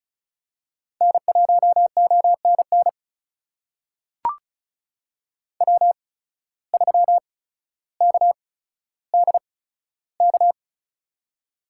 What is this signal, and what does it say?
Morse code, heard as a keyed tone, about 36 wpm, spaced as for 15 wpm, 700 Hz. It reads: N1ONN W3KDK
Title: USA Call Signs - 5 Characters - 8x Character Spacing - 35wpm